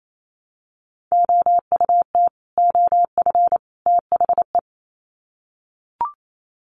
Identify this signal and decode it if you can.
Morse code, heard as a keyed tone, about 28 wpm, 700 Hz. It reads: OUT OF THE
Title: Sets of 3 Words 28wpm